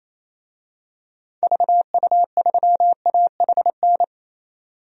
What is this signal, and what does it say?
Morse code, heard as a keyed tone, about 27 wpm, 700 Hz. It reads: VU3AHN
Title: DXCC-100 Call Signs - 6 Characters - 28wpm